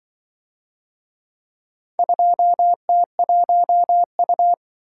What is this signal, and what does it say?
Morse code, heard as a keyed tone, about 24 wpm, 700 Hz. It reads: 2T1U
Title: Speed-Racing - Four Letter-Number Combinations - 20wpm